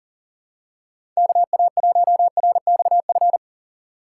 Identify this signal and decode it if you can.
Morse code, heard as a keyed tone, about 40 wpm, 700 Hz. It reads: KA1RXF
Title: USA Call Signs - 6 Characters - 2x Character Spacing - 40wpm